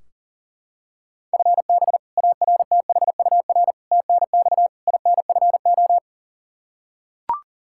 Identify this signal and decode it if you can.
Morse code, heard as a keyed tone, about 40 wpm, 700 Hz. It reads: FB ARTHUR TNX INFO
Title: Practice QSO - Exchange 3 - More Info - 40wpm